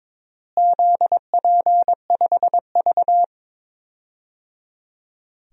Morse code, heard as a keyed tone, about 22 wpm, 700 Hz. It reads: ZP5V
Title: DXCC-100 Call Signs - 4 Characters - Repeated 3x - 22wpm